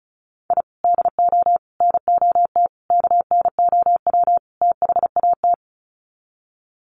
Morse code, heard as a keyed tone, about 34 wpm, 700 Hz. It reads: I DO NOT KNOW THAT
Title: Sets of 5 Words 35wpm